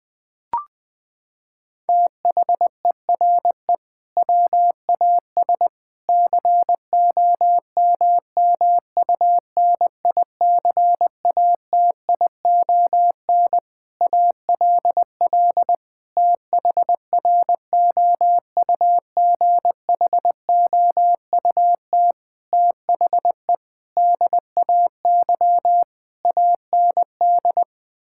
Morse code, heard as a keyed tone, about 20 wpm, 700 Hz. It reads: THERE WAS COMMUNICATION ALL THROUGHOUT THE DAY AND
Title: Sentences from Top 1000 Words - Review Part 6 - 20wpm